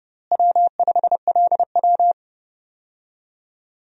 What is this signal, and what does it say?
Morse code, heard as a keyed tone, about 30 wpm, 700 Hz. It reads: W5LW
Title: USA Call Signs - 4 Characters - Repeated 3x - 30wpm